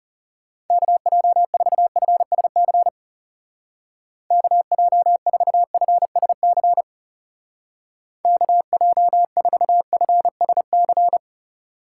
Morse code, heard as a keyed tone, about 34 wpm, 700 Hz. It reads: KJ4FSC KJ4FSC KJ4FSC
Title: Speed Racing - USA 2x3 Call Signs 30wpm